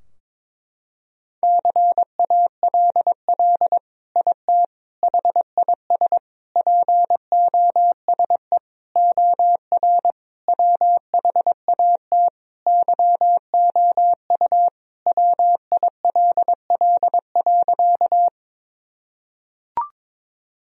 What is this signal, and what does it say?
Morse code, heard as a keyed tone, about 22 wpm, 700 Hz. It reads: CALL IT HIS POSE OR WHAT YOU WILL.
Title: Sentences from Top 1000 Words - Review Part 2 - 22wpm